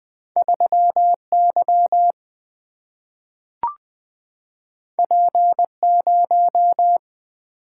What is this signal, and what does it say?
Morse code, heard as a keyed tone, about 20 wpm, 700 Hz. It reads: 3 Y P 0
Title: Single-Letter Single-Number 20wpm